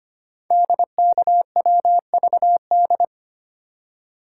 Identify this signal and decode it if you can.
Morse code, heard as a keyed tone, about 25 wpm, 700 Hz. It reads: DKWVD
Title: Code Groups - 5 Letters - 25wpm